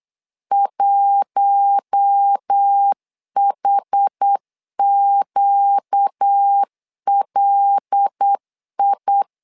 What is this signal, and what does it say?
Morse code, heard as a keyed tone, about 8 wpm, 800 Hz. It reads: 1HQLI
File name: LEZIONI / 5 LBVC / 5 LB 8wpm 2816